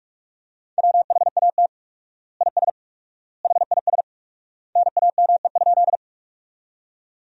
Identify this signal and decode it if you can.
Morse code, heard as a keyed tone, about 43 wpm, 700 Hz. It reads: WHAT IS HIS NAME?
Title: 4x Word-Spacing - Sentences from Top 200 Words - 45wpm